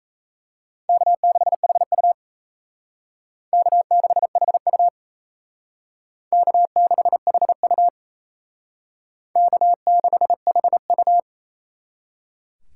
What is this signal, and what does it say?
Morse code, heard as a keyed tone, about 35 wpm, 700 Hz. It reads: K6HU K6HU K6HU K6HU